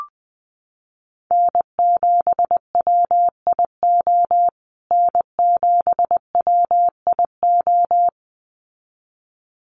Morse code, heard as a keyed tone, about 20 wpm, 700 Hz. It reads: N7WIO N7WIO
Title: USA Call Signs Repeated 2x 20wpm